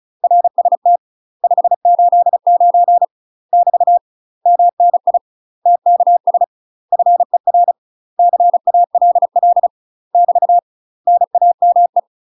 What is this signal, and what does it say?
Morse code, heard as a keyed tone, about 36 wpm, 700 Hz. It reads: RST 589 <BT> MNI TKS FER CALL <BT> NAME
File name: UK MASTER QSOs TRAINING / 35WPM / UK MASTER QSOs 0028